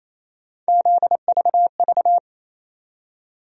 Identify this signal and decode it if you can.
Morse code, heard as a keyed tone, about 28 wpm, 700 Hz. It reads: ZVV
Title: USA Call Sign Postfixes - 3 Letters 28wpm